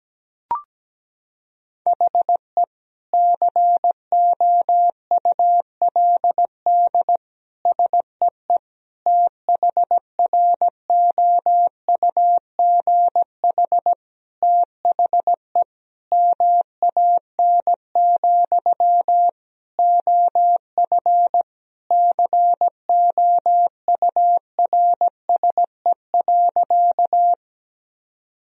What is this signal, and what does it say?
Morse code, heard as a keyed tone, about 17 wpm, 700 Hz. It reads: HE COULD SEE THROUGH THE MAN, OF COURSE.
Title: Sentences from Top 1000 Words - Review Part 1 - 17wpm